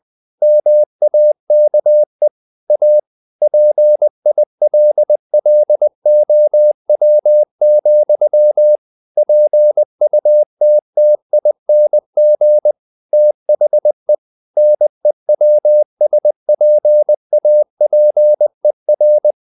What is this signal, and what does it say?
Morse code, heard as a keyed tone, about 20 wpm, 600 Hz. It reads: MAKE A PILLOW, PUTTING THE NEWSPAPER
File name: the old man and the sea / the old man and the sea part 1 20wpm